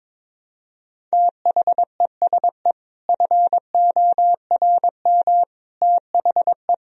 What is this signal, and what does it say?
Morse code, heard as a keyed tone, about 22 wpm, 700 Hz. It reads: THESE FORM THE
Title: Sentences from Top 1000 Words - Review Part 1 - 22wpm